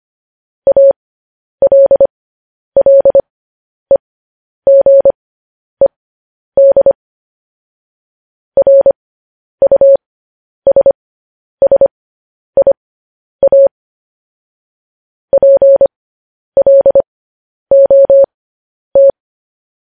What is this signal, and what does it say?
Morse code, heard as a keyed tone, about 25 wpm, spaced as for 10 wpm, 550 Hz. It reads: ALLEGED RUSSIA PLOT